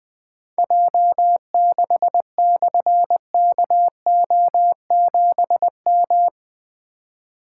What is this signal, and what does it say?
Morse code, heard as a keyed tone, about 20 wpm, 700 Hz. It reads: J6/KO7M
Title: Difficult Call Signs 20wpm